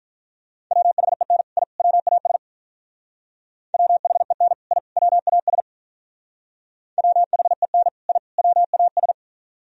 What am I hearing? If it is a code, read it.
Morse code, heard as a keyed tone, about 45 wpm, 700 Hz. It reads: WHEN I WAS WHEN I WAS WHEN I WAS